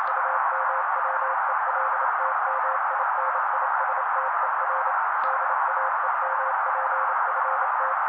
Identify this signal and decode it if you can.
Morse code, heard as a keyed tone, about 28 wpm, 530 Hz. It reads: UMWELTMINISTER DAEMPFT
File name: daily / 27wpm noise